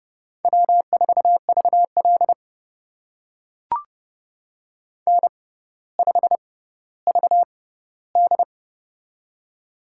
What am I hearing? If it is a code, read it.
Morse code, heard as a keyed tone, about 30 wpm, spaced as for 15 wpm, 700 Hz. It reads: W4VL N5VD
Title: USA Call Signs - 4 Characters - 6x Character Spacing - 30wpm